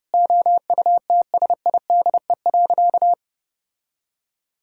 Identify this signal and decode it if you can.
Morse code, heard as a keyed tone, about 31 wpm, 700 Hz. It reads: OUTSIDE.
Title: Sentences from Top 1000 Words - Review Part 6 - 30wpm